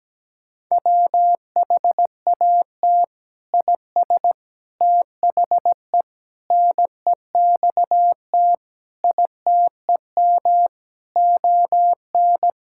Morse code, heard as a keyed tone, about 17 wpm, 700 Hz. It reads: WHAT IS THE NEXT ITEM ON
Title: Sentences from Top 1000 Words - Review Part 6 - 17wpm